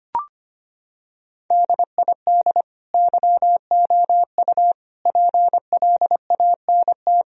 Morse code, heard as a keyed tone, about 25 wpm, 700 Hz. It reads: DID YOU PLANT
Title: Sentences from Top 1000 Words - Review Part 1 - 25wpm